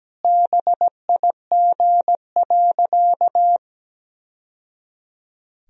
Morse code, heard as a keyed tone, about 17 wpm, 700 Hz. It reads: BIG.
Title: Sentences from Top 1000 Words - Review Part 2 - 17wpm